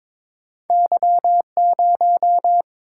Morse code, heard as a keyed tone, about 22 wpm, 700 Hz. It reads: Y0
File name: practice / 22wpm / Two Letter-Number Combinations - 22wpm